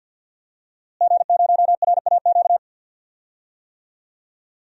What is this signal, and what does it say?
Morse code, heard as a keyed tone, about 52 wpm, 700 Hz. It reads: G0RAQ
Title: DXCC-100 Call Signs - 5 Characters - Repeated 2x - 50wpm